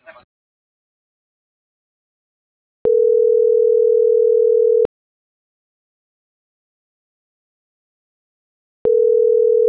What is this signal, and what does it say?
Ringback tone (North America), 440 + 480 Hz.
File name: CENTRAL MABE / out miriam.soledad - .64756